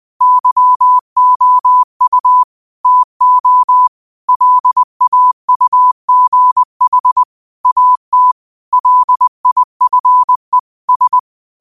Morse code, heard as a keyed tone, about 20 wpm, 1000 Hz. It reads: YOU TO LAUGH AT LIFE S
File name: qotdfiles / 20wpm